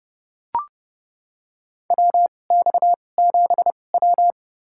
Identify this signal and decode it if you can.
Morse code, heard as a keyed tone, about 32 wpm, spaced as for 22 wpm, 700 Hz. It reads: WX7W
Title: USA Call Signs - 4 Characters - 2x Character Spacing - 30wpm